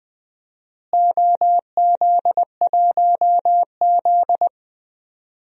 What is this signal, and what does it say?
Morse code, heard as a keyed tone, about 20 wpm, 700 Hz. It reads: OZ1Z
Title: DXCC-100 Call Signs - 4 Characters - Repeated 2x - 20wpm